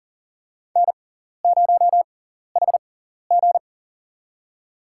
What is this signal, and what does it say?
Morse code, heard as a keyed tone, about 39 wpm, spaced as for 14 wpm, 690 Hz. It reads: N0HG
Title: USA Call Signs - 4 Characters - 6x Character Spacing - 40wpm